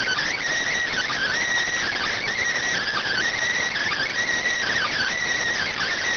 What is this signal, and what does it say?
Emergency Alert System (SAME) tones (header bursts), heard as harsh buzzing bursts. It is heard from 0.9 to 5.2 s.